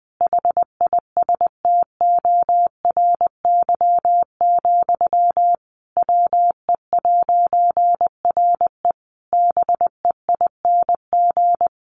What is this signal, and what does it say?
Morse code, heard as a keyed tone, about 20 wpm, 700 Hz. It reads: HISTORY, WE'RE BEING
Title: A Sound Of Thunder 20wpm